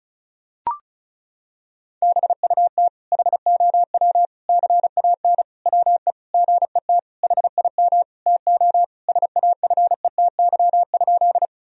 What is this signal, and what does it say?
Morse code, heard as a keyed tone, about 34 wpm, 700 Hz. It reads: BUT HOW CAN WE GET HIM TO SAFETY?
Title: Sentences from Top 1000 Words - Review Part 4 - 35wpm